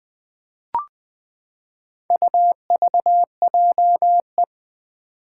Morse code, heard as a keyed tone, about 21 wpm, 700 Hz. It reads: UVJE
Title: Four Challenging Letter-Number Combinations 20wpm